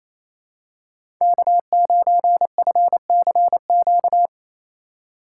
Morse code, heard as a keyed tone, about 28 wpm, 700 Hz. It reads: K9FCQ
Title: USA Call Signs - 5 Characters - 4x Character Spacing - 28wpm